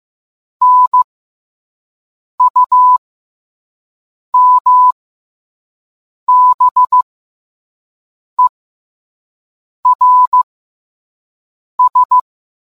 Morse code, heard as a keyed tone, about 15 wpm, spaced as for 5 wpm, 1000 Hz. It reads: NUMBERS